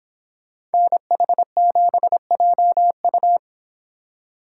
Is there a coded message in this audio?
Yes: Morse code from 0.7 to 3.4 s, heard as a keyed tone, about 29 wpm, 700 Hz. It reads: NH7JU